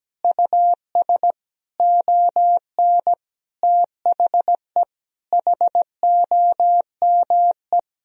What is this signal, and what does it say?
Morse code, heard as a keyed tone, about 17 wpm, 700 Hz. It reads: US ON THE HOME
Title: Sentences from Top 1000 Words - Review Part 2 - 17wpm